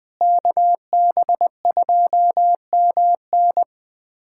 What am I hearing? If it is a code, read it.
Morse code, heard as a keyed tone, about 20 wpm, 700 Hz. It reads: KB2MN
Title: USA Callsign DE Callsign - 5 Characters - 20wpm